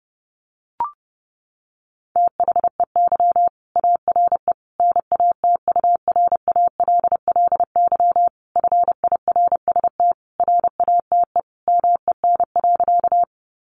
Morse code, heard as a keyed tone, about 30 wpm, 700 Hz. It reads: THEY ARE NATURALLY FIRST RATE MEN.
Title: Sentences from Top 1300 Words 30wpm